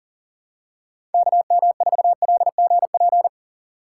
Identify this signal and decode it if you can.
Morse code, heard as a keyed tone, about 38 wpm, 700 Hz. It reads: KM4LGP
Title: USA Call Signs - 6 Characters - 3x Character Spacing - 40wpm